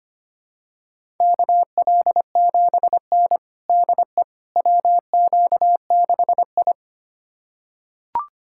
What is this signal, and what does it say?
Morse code, heard as a keyed tone, about 25 wpm, 700 Hz. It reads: KL7N DE WQ6I
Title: USA Callsign DE Callsign - 4 Characters - 25wpm